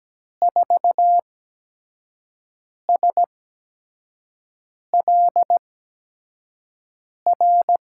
Morse code, heard as a keyed tone, about 17 wpm, spaced as for 5 wpm, 700 Hz. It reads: 4SLR
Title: USA Call Signs - 6 Characters - 8x Character Spacing - 17wpm